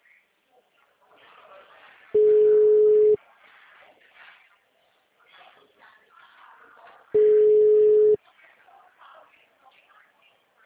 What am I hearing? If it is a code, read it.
Ringback tone (Europe, 425 Hz).